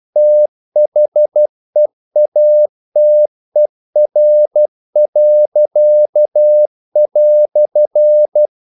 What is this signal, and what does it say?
Morse code, heard as a keyed tone, about 12 wpm, 600 Hz. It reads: THEATER."
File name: CW / Practice / EnglishQuotations0312